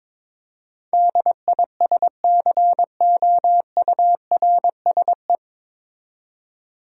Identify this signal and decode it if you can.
Morse code, heard as a keyed tone, about 22 wpm, 700 Hz. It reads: DISCOURSE
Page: Morse Code Ninja practice 22wpm